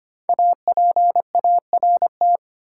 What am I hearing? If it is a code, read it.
Morse code, heard as a keyed tone, about 25 wpm, 700 Hz. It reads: APART
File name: 5 Letter Words - First 3 Letters Given - 25wpm